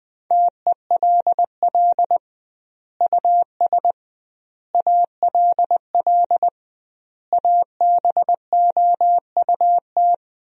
Morse code, heard as a keyed tone, about 20 wpm, 700 Hz. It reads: TELL US ALL ABOUT
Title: 2x Word-Spacing - Sentences from Top 200 Words - 20wpm